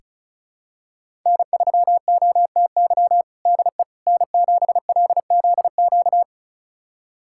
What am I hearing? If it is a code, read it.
Morse code, heard as a keyed tone, about 36 wpm, 700 Hz. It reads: N3OTY DE N7LZQ
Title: USA Callsign DE Callsign - 5 Characters - 35wpm